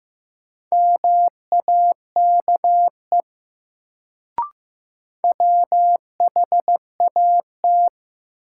Morse code, heard as a keyed tone, about 15 wpm, 700 Hz. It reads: MAKE WHAT
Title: Sets of 2 Words - New Word Drill - 15wpm